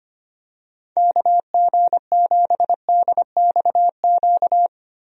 Morse code, heard as a keyed tone, about 25 wpm, 700 Hz. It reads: KG7DXQ
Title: USA Call Signs - 6 Characters - Repeated 2x - 25wpm